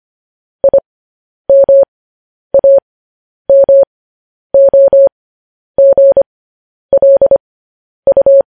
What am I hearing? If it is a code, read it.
Morse code, heard as a keyed tone, about 25 wpm, spaced as for 10 wpm, 550 Hz. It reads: IMAMOGLU
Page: news-headlines-in-morse-code-at-10-wpm